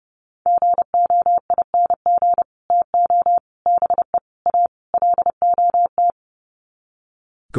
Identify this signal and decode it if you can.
Morse code, heard as a keyed tone, about 30 wpm, 700 Hz. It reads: GOING TO BE A LOT
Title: Sets of 5 Words Encore 30wpm